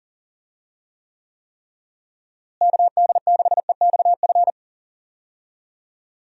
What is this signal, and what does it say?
Morse code, heard as a keyed tone, about 40 wpm, 700 Hz. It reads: KD6EXF
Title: USA Call Signs - 6 Characters - Repeated 3x - 40wpm